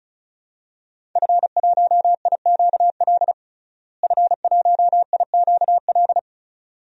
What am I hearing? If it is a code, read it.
Morse code, heard as a keyed tone, about 36 wpm, 700 Hz. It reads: F1IQL F1IQL
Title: DXCC-100 Call Signs - 5 Characters - Repeated 2x - 35wpm